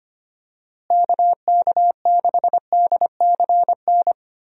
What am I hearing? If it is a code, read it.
Morse code, heard as a keyed tone, about 25 wpm, 700 Hz. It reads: KK6DCN
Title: USA Call Signs - 6 Characters - Repeated 3x - 25wpm